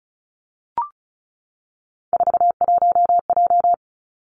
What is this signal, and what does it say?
Morse code, heard as a keyed tone, about 36 wpm, 700 Hz. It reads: 41J
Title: Three Challenging Letter-Number Combinations 35wpm